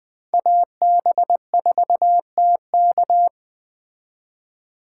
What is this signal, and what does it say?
Morse code, heard as a keyed tone, about 24 wpm, 700 Hz. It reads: AB4TK
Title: Speed Racing - USA 2x2 Call Signs 20wpm